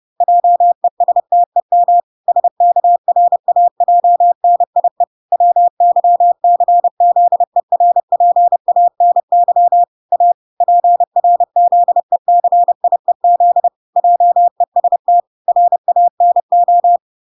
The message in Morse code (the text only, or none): JESTEM SKRAJNIE WYCZERPANY A PRZECIEZ JEST RANO